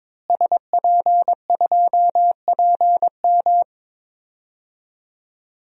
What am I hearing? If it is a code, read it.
Morse code, heard as a keyed tone, about 22 wpm, 700 Hz. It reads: SP2PM
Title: DXCC-100 Call Signs - 5 Characters - Repeated 3x - 22wpm